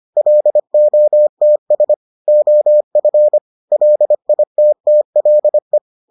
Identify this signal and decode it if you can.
Morse code, heard as a keyed tone, about 25 wpm, 600 Hz. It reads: LOTS OF LITTLE